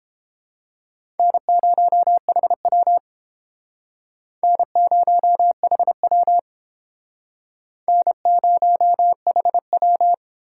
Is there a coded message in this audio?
Yes: Morse code from 1.2 to 10.1 s, heard as a keyed tone, about 31 wpm, 700 Hz. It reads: N0HW N0HW N0HW